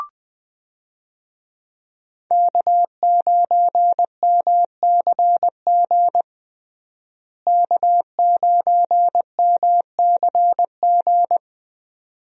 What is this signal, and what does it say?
Morse code, heard as a keyed tone, about 20 wpm, 700 Hz. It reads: K9MCG K9MCG
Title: USA Call Signs - 5 Characters - Repeated 2x - 20wpm